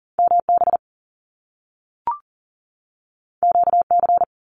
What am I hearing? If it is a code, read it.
Morse code, heard as a keyed tone, about 41 wpm, 700 Hz. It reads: MB QC
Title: Canadian Provinces and Territories 40wpm